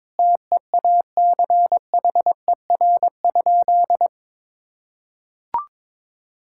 Morse code, heard as a keyed tone, about 22 wpm, 700 Hz. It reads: TEACHER?
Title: Sentences from Top 1000 Words - Review Part 6 - 22wpm